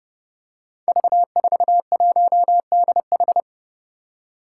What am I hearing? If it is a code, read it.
Morse code, heard as a keyed tone, about 28 wpm, 700 Hz. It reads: V41DH
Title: Five Challenging Letter-Number Combinations 30wpm